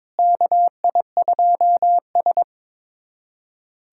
Morse code, heard as a keyed tone, about 22 wpm, 700 Hz. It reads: KI2S
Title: USA Callsign DE Callsign - 4 Characters - 22wpm